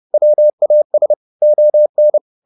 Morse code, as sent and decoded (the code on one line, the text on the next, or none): .-- .- ... / --- -.
WAS ON